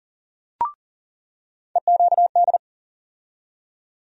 Morse code, heard as a keyed tone, about 40 wpm, 700 Hz. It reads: EQD
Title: USA Call Sign Postfixes - 3 Letters 40wpm